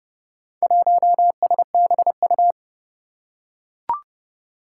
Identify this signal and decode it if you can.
Morse code, heard as a keyed tone, about 30 wpm, 700 Hz. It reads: 1SBU
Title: Four Challenging Letter-Number Combinations 30wpm